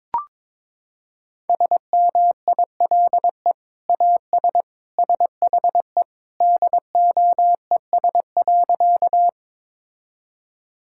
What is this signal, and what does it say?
Morse code, heard as a keyed tone, about 21 wpm, 700 Hz. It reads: SMILE AS SHE DOES.
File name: practice / 22wpm / Sentences with 2-Word Phrases - 22wpm